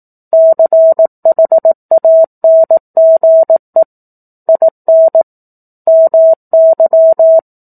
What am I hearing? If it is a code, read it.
Morse code, heard as a keyed tone, about 18 wpm, 650 Hz. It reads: CHANGE IN MY
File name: 04-Chapter-04 18 WPM